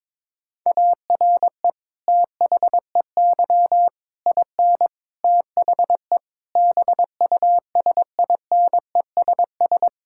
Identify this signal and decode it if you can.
Morse code, heard as a keyed tone, about 22 wpm, 700 Hz. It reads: ARE THEY IN THE BUSINESS
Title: Sentences from Top 1600 Words 22wpm